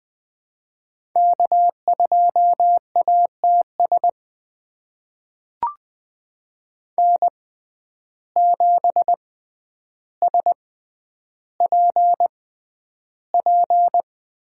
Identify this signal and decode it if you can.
Morse code, heard as a keyed tone, about 20 wpm, spaced as for 10 wpm, 700 Hz. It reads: K2ATS N7SPP
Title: USA Call Signs - 5 Characters - 6x Character Spacing - 20wpm